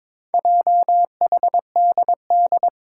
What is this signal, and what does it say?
Morse code, heard as a keyed tone, about 22 wpm, 700 Hz. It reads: JHDD
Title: Four Challenging Letter-Number Combinations 22wpm